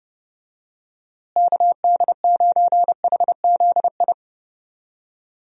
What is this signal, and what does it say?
Morse code, heard as a keyed tone, about 30 wpm, 700 Hz. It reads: KD9HZI
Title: USA Call Signs - 6 Characters - Repeated 2x - 30wpm